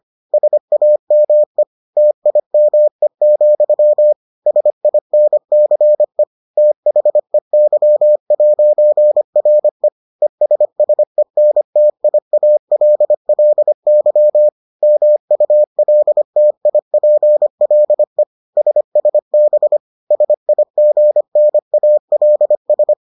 Morse code, heard as a keyed tone, about 25 wpm, 600 Hz. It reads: SAME TIME, SINCE THEY'RE ESSENTIALLY MULTIPLE SSB SIGNALS